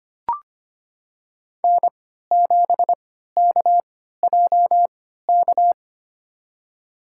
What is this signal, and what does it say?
Morse code, heard as a keyed tone, about 25 wpm, spaced as for 14 wpm, 700 Hz. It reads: N7KJK
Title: USA Call Signs - 5 Characters - 3x Character Spacing - 25wpm